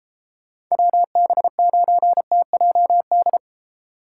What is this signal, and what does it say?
Morse code, heard as a keyed tone, about 34 wpm, 700 Hz. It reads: WB9TJD
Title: Speed Racing - USA Call Signs 28wpm